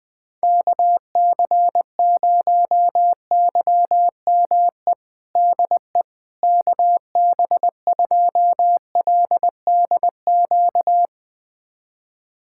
Morse code, heard as a keyed tone, about 20 wpm, 700 Hz. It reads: KC0YME DE KB2LDQ
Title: USA Callsign DE Callsign - 6 Characters - 20wpm